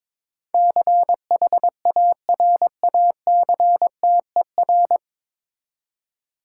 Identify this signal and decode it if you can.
Morse code, heard as a keyed tone, about 22 wpm, 700 Hz. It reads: CHARACTER
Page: Morse Code Ninja practice 22wpm